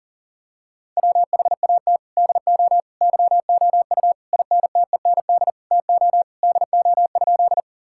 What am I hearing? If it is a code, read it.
Morse code, heard as a keyed tone, about 40 wpm, 700 Hz. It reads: WHAT DO YOU INTEND TO DO?
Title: Sentences from Top 1600 Words 40wpm